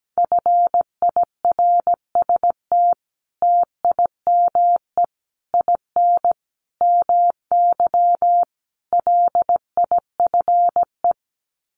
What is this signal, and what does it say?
Morse code, heard as a keyed tone, about 17 wpm, 700 Hz. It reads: FIRST TIME IN MY LIFE
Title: Sets of 5 Words Encore2 17wpm